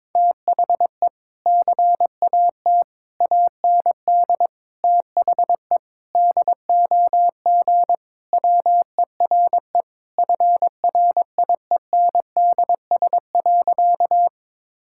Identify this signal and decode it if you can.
Morse code, heard as a keyed tone, about 22 wpm, 700 Hz. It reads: THE CAT AND THE DOG WERE FRIENDS.
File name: practice / 22wpm / Sentences with 2-Word Phrases - 22wpm